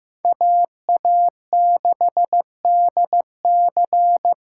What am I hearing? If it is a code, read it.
Morse code, heard as a keyed tone, about 15 wpm, 700 Hz. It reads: AA6DC
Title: USA Call Signs - 5 Characters - 8x Character Spacing - 15wpm